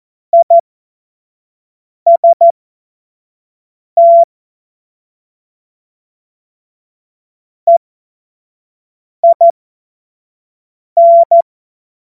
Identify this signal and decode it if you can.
Morse code, heard as a keyed tone, about 14 wpm, spaced as for 5 wpm, 680 Hz. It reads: IST EIN
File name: Morsekurs / Witze / Witz 16 April 2024 25 F70